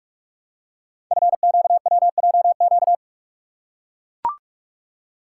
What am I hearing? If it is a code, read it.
Morse code, heard as a keyed tone, about 44 wpm, 700 Hz. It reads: FQWJQ
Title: Code Groups - 5 Letters - 45wpm